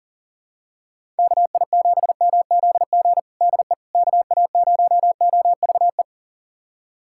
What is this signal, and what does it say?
Morse code, heard as a keyed tone, about 40 wpm, 700 Hz. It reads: KI7MZG DE KA0OVE
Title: USA Callsign DE Callsign 40wpm